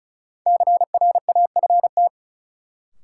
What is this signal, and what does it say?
Morse code, heard as a keyed tone, about 34 wpm, 700 Hz. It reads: CRAFT
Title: Sentences from Top 1000 Words - Review Part 6 - New Word Drill - 35wpm